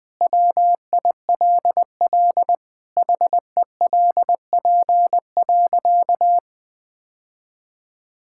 Morse code, heard as a keyed tone, about 20 wpm, 700 Hz. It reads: WILL HELP.
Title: Sentences with 3-Word Phrases - Spoken Beginning - 20wpm